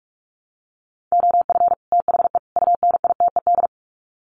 Morse code, heard as a keyed tone, about 44 wpm, 700 Hz. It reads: OF THE UNITED
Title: Sets of 3 Words 45wpm